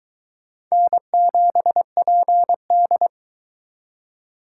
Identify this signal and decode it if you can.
Morse code, heard as a keyed tone, about 22 wpm, 700 Hz. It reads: N7PD